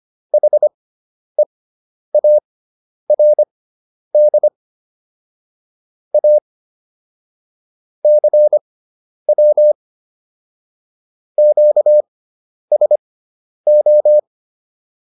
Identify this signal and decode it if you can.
Morse code, heard as a keyed tone, about 25 wpm, spaced as for 10 wpm, 600 Hz. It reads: HEARD A CW QSO